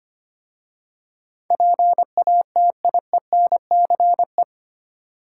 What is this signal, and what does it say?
Morse code, heard as a keyed tone, about 25 wpm, 700 Hz. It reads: PATIENCE
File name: 8 Letter Words - First 3 Letters Given - 25wpm